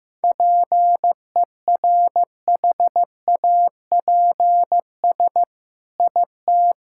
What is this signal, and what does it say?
Morse code, heard as a keyed tone, about 15 wpm, 700 Hz. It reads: PERHAPS IT
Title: Sentences from Top 1000 Words - Review Part 5 - 15wpm